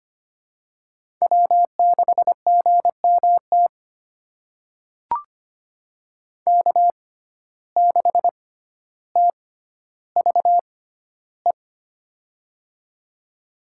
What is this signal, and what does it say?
Morse code, heard as a keyed tone, about 25 wpm, spaced as for 13 wpm, 700 Hz. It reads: W6GMT K6TVE
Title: USA Call Signs - 5 Characters - 6x Character Spacing - 25wpm